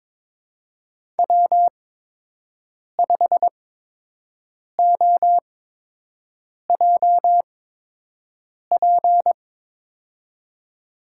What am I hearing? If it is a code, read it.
Morse code, heard as a keyed tone, about 22 wpm, spaced as for 6 wpm, 700 Hz. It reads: W5OJP
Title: USA Call Signs - 8x Character Spacing - 22wpm